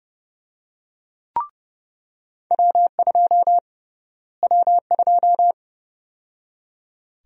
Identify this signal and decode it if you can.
Morse code, heard as a keyed tone, about 30 wpm, 700 Hz. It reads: W2 W2
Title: USA Call Sign Prefixes - Repeated 2x - 30wpm